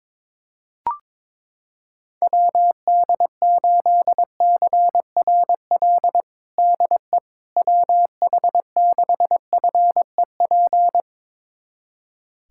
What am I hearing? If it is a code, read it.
Morse code, heard as a keyed tone, about 22 wpm, 700 Hz. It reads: WD8CRL DE WH6FEP
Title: USA Callsign DE Callsign - 6 Characters - 22wpm